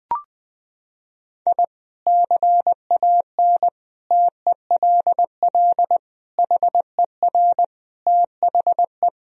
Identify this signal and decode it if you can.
Morse code, heard as a keyed tone, about 20 wpm, 700 Hz. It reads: I CAN TELL HER THE
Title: Sentences from Top 1000 Words - Review Part 3 - 20wpm